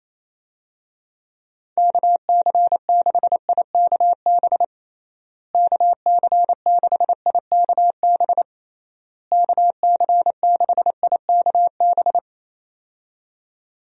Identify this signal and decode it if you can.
Morse code, heard as a keyed tone, about 28 wpm, 700 Hz. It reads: KC6IKB KC6IKB KC6IKB
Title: USA Call Signs - 6 Characters - Repeated 3x - 28wpm